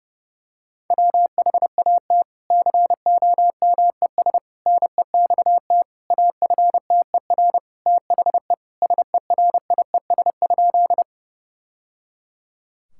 Morse code, heard as a keyed tone, about 30 wpm, 700 Hz. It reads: WHAT COMES NEXT AFTER THE SERIES?
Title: Sentences from Top 1000 Words - Review Part 5 - 30wpm